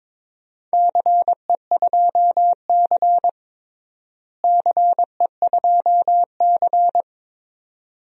Morse code, heard as a keyed tone, about 22 wpm, 700 Hz. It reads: CE2C CE2C
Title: DXCC-100 Call Signs - 4 Characters - Repeated 2x - 22wpm